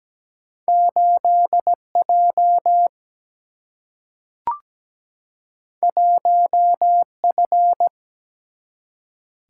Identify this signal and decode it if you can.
Morse code, heard as a keyed tone, about 17 wpm, 700 Hz. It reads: 8 J 1 F
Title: Single-Letter Single-Number 17wpm